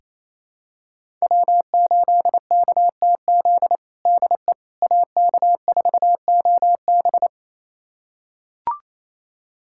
Morse code, heard as a keyed tone, about 28 wpm, 700 Hz. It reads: W8KTZ DE AK4OB
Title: USA Callsign DE Callsign - 5 Characters - 28wpm